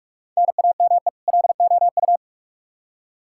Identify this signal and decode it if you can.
Morse code, heard as a keyed tone, about 46 wpm, 700 Hz. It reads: NAME LOU
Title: Practice QSO - Exchange 1 - Beginning - 45wpm